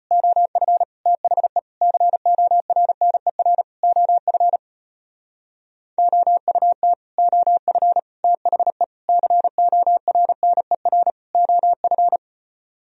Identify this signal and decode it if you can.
Morse code, heard as a keyed tone, about 33 wpm, 700 Hz. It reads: OF THE CORNER OF OUT OF THE CORNER OF